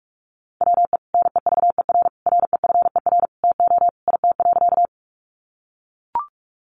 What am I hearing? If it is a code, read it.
Morse code, heard as a keyed tone, about 44 wpm, 700 Hz. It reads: WE NEVER REFER TO IT.
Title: Sentences from Top 1000 Words 45wpm